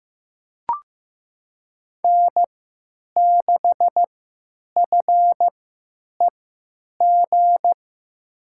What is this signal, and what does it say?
Morse code, heard as a keyed tone, about 15 wpm, spaced as for 9 wpm, 700 Hz. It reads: N6FEG
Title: USA Call Signs - 5 Characters - 3x Character Spacing - 15wpm